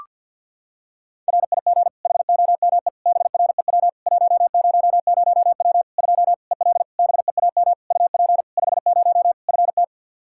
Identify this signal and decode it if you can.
Morse code, heard as a keyed tone, about 52 wpm, 700 Hz. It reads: RIG HOME BREW 100W 2 EL BEAM UP 50 FT
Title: Practice QSO - Exchange 2 - Info - 50wpm